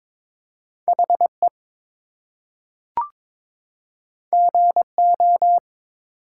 Morse code, heard as a keyed tone, about 22 wpm, 700 Hz. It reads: HE GO
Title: Two Letter Words - 22wpm